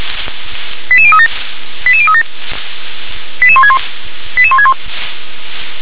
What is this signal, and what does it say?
Five-tone selective call (ZVEI), heard as quick runs of five tones, tones of about 70 ms. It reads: 88037 88037 88161 88161